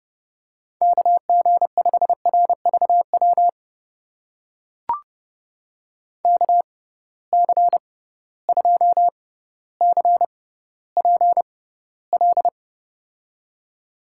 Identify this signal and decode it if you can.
Morse code, heard as a keyed tone, about 30 wpm, spaced as for 15 wpm, 700 Hz. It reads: KG5RVW KC2CPL
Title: USA Call Signs - 6 Characters - 6x Character Spacing - 30wpm